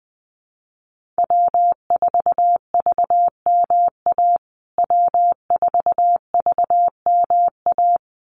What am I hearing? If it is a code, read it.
Morse code, heard as a keyed tone, about 19 wpm, 700 Hz. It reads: W4VMA W4VMA
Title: USA Call Signs Repeated 2x 20wpm